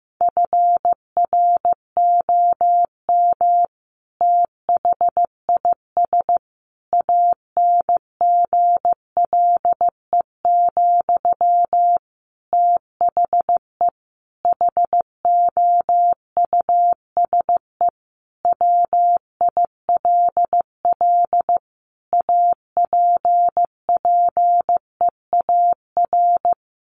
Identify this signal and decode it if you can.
Morse code, heard as a keyed tone, about 15 wpm, 700 Hz. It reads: FROM THIS ANGLE, THE HOUSE WILL APPEAR
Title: Sentences from Top 1500 Words 15wpm